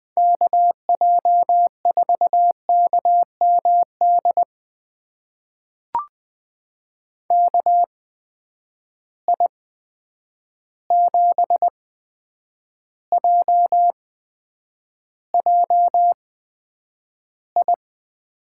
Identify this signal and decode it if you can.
Morse code, heard as a keyed tone, about 20 wpm, spaced as for 9 wpm, 700 Hz. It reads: KJ4KMD KI7JJI
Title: USA Call Signs - 6 Characters - 8x Character Spacing - 20wpm